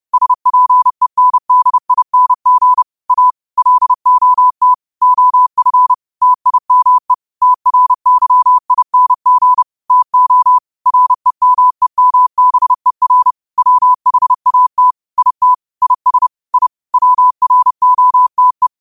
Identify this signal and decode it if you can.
Morse code, heard as a keyed tone, about 30 wpm, 1000 Hz. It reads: SPENDING A LOT OF TIME TRYING TO REMEMBER WHAT IT IS I WROTE
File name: qotdfiles / 30wpm